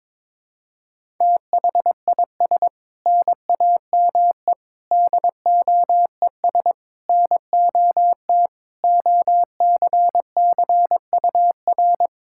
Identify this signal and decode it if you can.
Morse code, heard as a keyed tone, about 22 wpm, 700 Hz. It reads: THIS NAME DOES NOT OCCUR
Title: Sentences from Top 1600 Words 22wpm